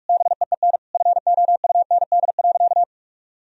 Morse code, heard as a keyed tone, about 43 wpm, 700 Hz. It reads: BEEN FOUND.
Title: Sentences from Top 1000 Words - Review Part 4 - 45wpm